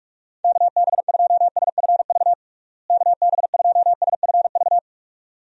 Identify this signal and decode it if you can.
Morse code, heard as a keyed tone, about 47 wpm, 700 Hz. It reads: KB2SFV KB2SFV
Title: USA Call Signs - 6 Characters - Repeated 2x - 45wpm